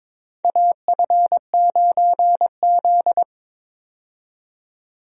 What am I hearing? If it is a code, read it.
Morse code, heard as a keyed tone, about 22 wpm, 700 Hz. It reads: AF9Z
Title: USA Call Signs - 4 Characters - 22wpm